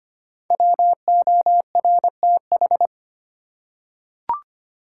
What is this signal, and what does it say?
Morse code, heard as a keyed tone, about 25 wpm, 700 Hz. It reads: WORTH
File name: 5 Letter Words - First 3 Letters Given - 25wpm